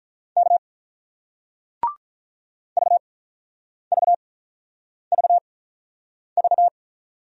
Morse code, heard as a keyed tone, about 47 wpm, spaced as for 9 wpm, 700 Hz. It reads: K VVVV